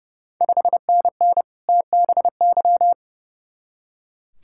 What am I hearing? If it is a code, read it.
Morse code, heard as a keyed tone, about 30 wpm, 700 Hz. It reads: 5NN TBY
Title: Ontario QSO Party Exchange 30wpm